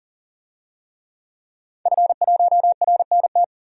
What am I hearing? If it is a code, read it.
Morse code, heard as a keyed tone, about 40 wpm, 700 Hz. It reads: F1RNT
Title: DXCC-100 Call Signs - 5 Characters - 40wpm